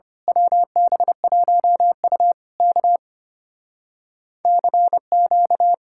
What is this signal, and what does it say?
Morse code, heard as a keyed tone, about 24 wpm, 700 Hz. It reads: WB1U K CQ
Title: Speed-Racing Common QSO Phrases - 25wpm